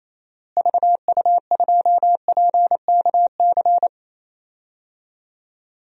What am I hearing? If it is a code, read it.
Morse code, heard as a keyed tone, about 28 wpm, 700 Hz. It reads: VU2PKC
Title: DXCC-100 Call Signs - 6 Characters - Repeated 3x - 28wpm